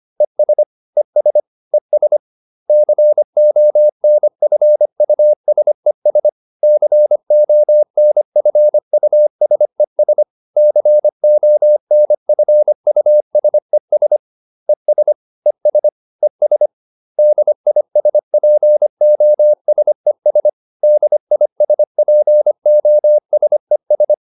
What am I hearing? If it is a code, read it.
Morse code, heard as a keyed tone, about 25 wpm, 600 Hz. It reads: ES ES ES CONFUSES CONFUSES CONFUSES ES ES ES DISPOSES DISPOSES